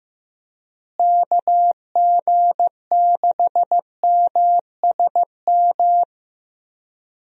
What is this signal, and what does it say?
Morse code, heard as a keyed tone, about 16 wpm, 700 Hz. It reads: KG6MSM
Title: USA Call Signs - 6 Characters - Repeated 3x - 15wpm